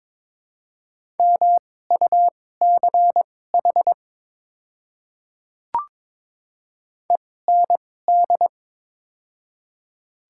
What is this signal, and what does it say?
Morse code, heard as a keyed tone, about 22 wpm, spaced as for 16 wpm, 700 Hz. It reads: MUCH END
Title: 2x Character Spacing - Top 300 Words - 22wpm